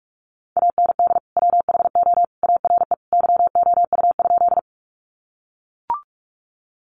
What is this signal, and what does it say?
Morse code, heard as a keyed tone, about 44 wpm, 700 Hz. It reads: AND WHO ARE YOU?
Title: Sentences from Top 100 Words 45wpm